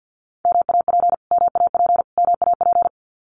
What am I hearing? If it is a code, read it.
Morse code, heard as a keyed tone, about 50 wpm, 700 Hz. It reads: MAP MAP MAP
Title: Top 500 Words Repeated 50wpm